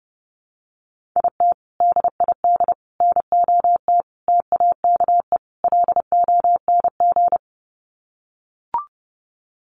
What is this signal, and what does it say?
Morse code, heard as a keyed tone, about 30 wpm, 700 Hz. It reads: IT DID NOT TAKE LONG
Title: Sets of 5 Words Encore2 30wpm